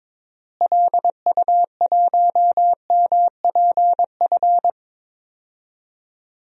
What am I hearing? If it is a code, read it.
Morse code, heard as a keyed tone, about 22 wpm, 700 Hz. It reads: LU1MPF
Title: DXCC-100 Call Signs - 6 Characters - Repeated 2x - 22wpm